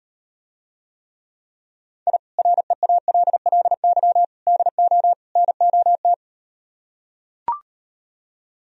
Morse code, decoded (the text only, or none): I REALLY DO NOT